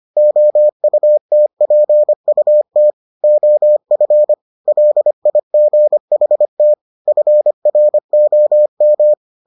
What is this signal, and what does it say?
Morse code, heard as a keyed tone, about 25 wpm, 600 Hz. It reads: OUTPUT OF LIGHT FROM